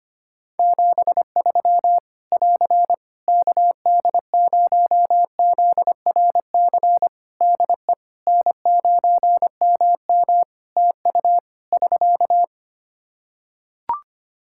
Morse code, heard as a keyed tone, about 25 wpm, 700 Hz. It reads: 73 <AR> KD0ZRC DE N9MM TU <SK>
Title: Practice QSO - Exchange 4 - Ending - 25wpm